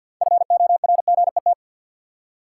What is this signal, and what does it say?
Morse code, heard as a keyed tone, about 53 wpm, 700 Hz. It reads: FORGET
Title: Six Letter Words - 50wpm